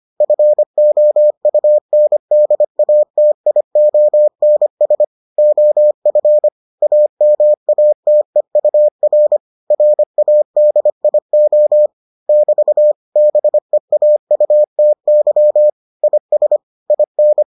Morse code, heard as a keyed tone, about 25 wpm, 600 Hz. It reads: FOUNDATIONS OF AMATEUR RADIO <BT> BEAUTY IS IN